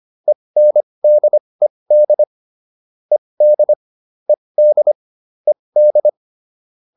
Morse code, heard as a keyed tone, about 25 wpm, spaced as for 20 wpm, 600 Hz. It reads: ENDED ED ED ED